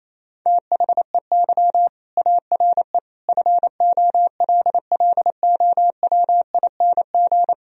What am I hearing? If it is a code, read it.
Morse code, heard as a keyed tone, about 27 wpm, 700 Hz. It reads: THEY ARE FOLLOWING
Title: Sentences from Top 1000 Words - Review Part 4 - 28wpm